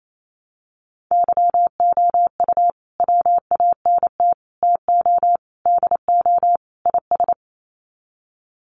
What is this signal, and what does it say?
Morse code, heard as a keyed tone, about 28 wpm, 700 Hz. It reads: YOU WANT TO DO IS
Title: Sets of 5 Words Encore2 28wpm